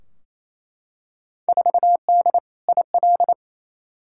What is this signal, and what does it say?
Morse code, heard as a keyed tone, about 29 wpm, 700 Hz. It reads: 4D IL
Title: Field Day Exchanges Encore - 28wpm